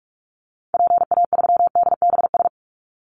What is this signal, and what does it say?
Morse code, heard as a keyed tone, about 46 wpm, 700 Hz. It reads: PA3DBS
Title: Call Signs - 45wpm